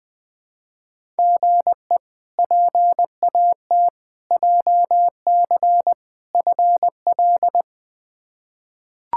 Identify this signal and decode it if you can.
Morse code, heard as a keyed tone, about 20 wpm, 700 Hz. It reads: GE PAT JC FL
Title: K1USN SST QSO Practice - 20wpm